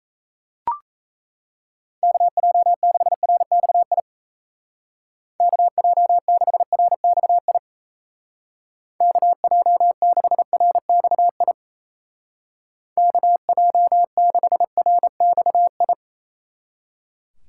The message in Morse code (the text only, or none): KJ6RXI KJ6RXI KJ6RXI KJ6RXI